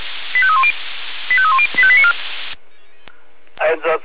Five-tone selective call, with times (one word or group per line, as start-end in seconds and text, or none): 0.4-0.7 85310
1.3-1.7 85310
1.8-2.1 85894